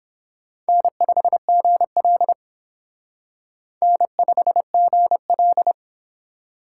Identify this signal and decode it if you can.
Morse code, heard as a keyed tone, about 26 wpm, 700 Hz. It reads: N5GL N5GL
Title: Speed Racing - USA Call Signs 22wpm